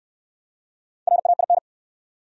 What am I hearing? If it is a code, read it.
Morse code, heard as a keyed tone, about 69 wpm, 700 Hz. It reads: RAIN